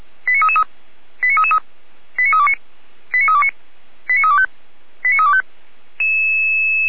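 Five-tone selective call (ZVEI), heard as quick runs of five tones, tones of about 70 ms. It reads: 89333 89333 89329 89329 89325 89325
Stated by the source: ZVEI five-tone call 89509 received by SDR.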